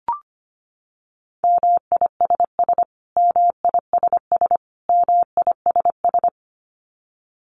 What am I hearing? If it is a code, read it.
Morse code, heard as a keyed tone, about 25 wpm, 700 Hz. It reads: MISS MISS MISS
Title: Top 500 Words Repeated 25wpm